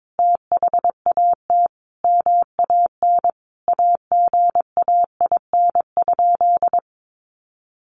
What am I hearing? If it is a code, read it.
Morse code, heard as a keyed tone, about 22 wpm, 700 Hz. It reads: THAT MAN AGAIN?
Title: Sentences from Top 1400 Words 22wpm